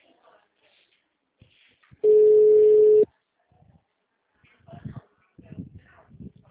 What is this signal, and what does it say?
Ringback tone (Europe, 425 Hz).